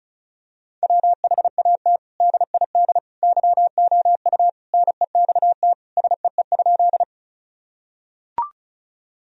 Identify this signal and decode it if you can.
Morse code, heard as a keyed tone, about 36 wpm, 700 Hz. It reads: WHAT DID YOU NEXT SEE?
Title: Sentences from Top 1000 Words - Review Part 1 - 35wpm